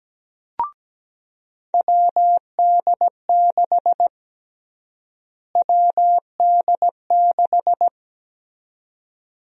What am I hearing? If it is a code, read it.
Morse code, heard as a keyed tone, about 17 wpm, 700 Hz. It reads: WD6 WD6
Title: USA Call Sign Prefixes - Repeated 2x - 17wpm